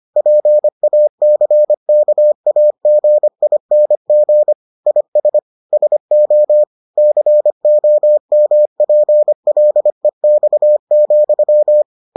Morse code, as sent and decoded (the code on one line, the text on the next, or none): .--. .- -.-. -.- .- --. .. -. --. / .. ... / ... --- / -.-. --- -- .--. .-.. -..- --..--
PACKAGING IS SO COMPLEX,